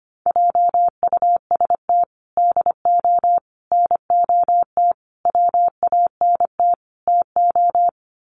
Morse code, heard as a keyed tone, about 25 wpm, 700 Hz. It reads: JUST DO NOT WANT TO
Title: Sets of 5 Words Encore 25wpm